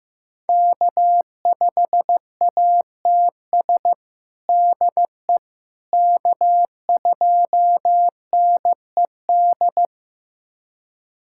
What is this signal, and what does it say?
Morse code, heard as a keyed tone, about 15 wpm, 700 Hz. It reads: K5ATS DE K2NED
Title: USA Callsign DE Callsign - 5 Characters - 15wpm